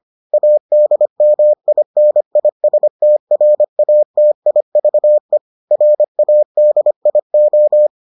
Morse code, heard as a keyed tone, about 25 wpm, 600 Hz. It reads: ADMINISTRATIVE RADIO